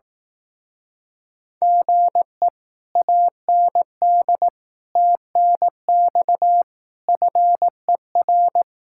Morse code, heard as a keyed tone, about 18 wpm, 700 Hz. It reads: GE AND TNX FER
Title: Speed-Racing Common QSO Phrases Encore - 15wpm